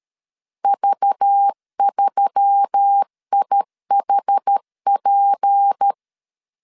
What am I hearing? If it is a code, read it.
Morse code, heard as a keyed tone, about 13 wpm, 800 Hz. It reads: V3IHP
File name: CW / DET / 1 6 1500C S B 12wpm 3431